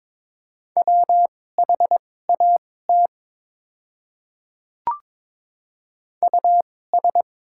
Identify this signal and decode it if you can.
Morse code, heard as a keyed tone, about 23 wpm, spaced as for 16 wpm, 700 Hz. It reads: WHAT US
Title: 2x Character Spacing - Top 200 Words - 22wpm